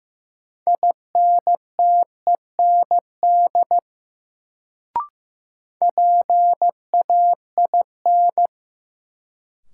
Morse code, heard as a keyed tone, about 15 wpm, 700 Hz. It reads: INTEND PAIN
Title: Sentences from Top 1600 Words - New Word Drill Alt - 15wpm